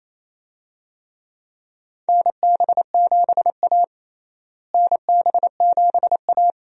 Morse code, heard as a keyed tone, about 28 wpm, 700 Hz. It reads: NB7A NB7A
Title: USA Call Signs - 4 Characters - Repeated 2x - 28wpm